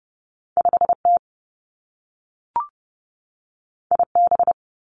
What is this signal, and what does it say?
Morse code, heard as a keyed tone, about 30 wpm, 700 Hz. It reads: 5 T I B
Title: Two Challenging Letter-Number Combinations 30wpm